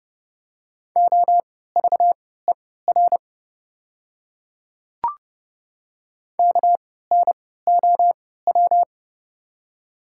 Morse code, heard as a keyed tone, about 30 wpm, spaced as for 17 wpm, 700 Hz. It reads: OVER KNOW
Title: 3x Character Spacing - Top 100 Words - 30wpm